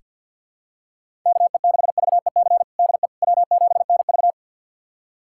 Morse code, heard as a keyed tone, about 47 wpm, 700 Hz. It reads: KE6VEC DE W7NV
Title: USA Callsign DE Callsign 50wpm